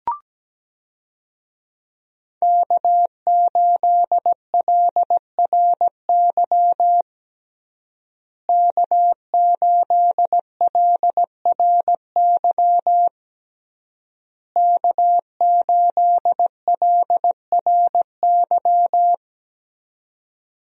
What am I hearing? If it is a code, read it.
Morse code, heard as a keyed tone, about 17 wpm, 700 Hz. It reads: K8LRY K8LRY K8LRY
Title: USA Call Signs - 5 Characters - Repeated 3x - 17wpm